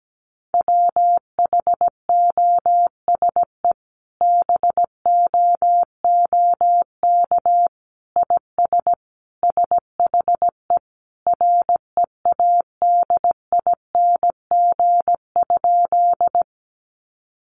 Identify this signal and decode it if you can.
Morse code, heard as a keyed tone, about 17 wpm, 700 Hz. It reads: WHOSE BOOK IS SHE READING?
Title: Sentences from Top 1200 Words 17wpm